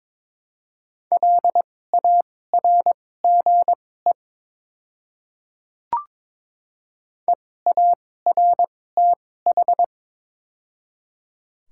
Morse code, heard as a keyed tone, about 22 wpm, spaced as for 16 wpm, 700 Hz. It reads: LARGE EARTH
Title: 2x Character Spacing - Top 300 Words - 22wpm